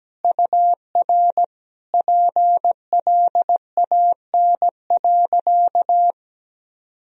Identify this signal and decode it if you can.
Morse code, heard as a keyed tone, about 17 wpm, 700 Hz. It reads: UR PLAN.
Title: Sentences from Top 500 Words and Abbreviations 17wpm